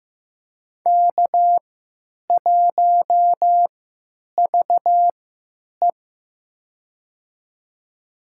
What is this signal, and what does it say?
Morse code, heard as a keyed tone, about 15 wpm, spaced as for 9 wpm, 700 Hz. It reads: K1VE
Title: USA Call Signs - 4 Characters - 3x Character Spacing - 15wpm